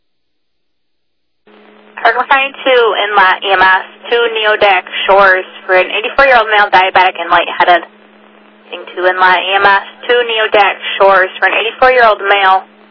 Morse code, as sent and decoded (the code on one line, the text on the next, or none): none